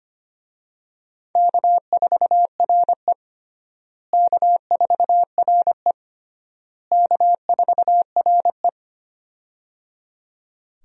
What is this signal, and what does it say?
Morse code, heard as a keyed tone, about 25 wpm, 700 Hz. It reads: K4RE K4RE K4RE
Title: USA Call Signs - 4 Characters - Repeated 3x - 25wpm